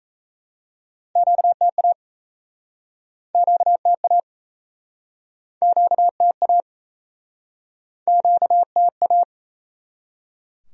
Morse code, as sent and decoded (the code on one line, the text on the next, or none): --.- - .- / --.- - .- / --.- - .- / --.- - .-
QTA QTA QTA QTA